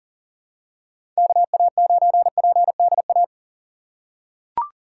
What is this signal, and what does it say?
Morse code, heard as a keyed tone, about 40 wpm, 700 Hz. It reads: KA9PDA
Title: USA Call Signs - 2x Character Spacing - 40wpm